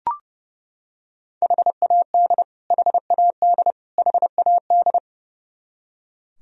Morse code, heard as a keyed tone, about 32 wpm, 700 Hz. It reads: HAD HAD HAD
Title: Top 100 Words Repeated 30wpm